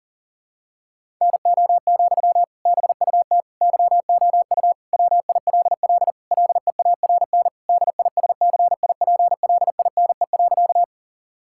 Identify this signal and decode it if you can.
Morse code, heard as a keyed tone, about 40 wpm, 700 Hz. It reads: NO, BUT YOU WILL LEARN DISCIPLINE.
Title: Sentences from Top 1000 Words - Review Part 6 - 40wpm